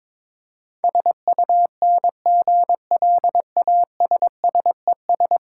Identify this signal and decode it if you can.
Morse code, heard as a keyed tone, about 24 wpm, 700 Hz. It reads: SUNGLASSES
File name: Easy Words - Speed-Racing with Hints - 22wpm